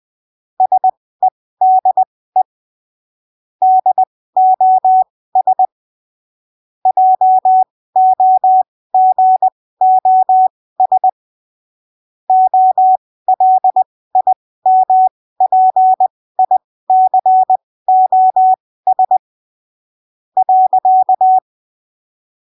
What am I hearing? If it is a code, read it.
Morse code, heard as a keyed tone, about 20 wpm, spaced as for 15 wpm, 750 Hz. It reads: SEDE DOS JOGOS OLIMPICOS .
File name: arquivos / texto / texto 08 20ppm